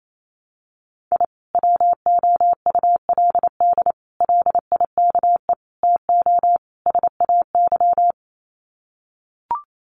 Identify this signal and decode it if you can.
Morse code, heard as a keyed tone, about 28 wpm, 700 Hz. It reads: I WOULD LIKE TO SAY
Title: Sets of 5 Words Encore2 28wpm